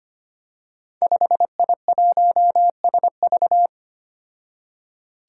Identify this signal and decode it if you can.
Morse code, heard as a keyed tone, about 25 wpm, 700 Hz. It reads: 5I1SV
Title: Five Challenging Letter-Number Combinations 25wpm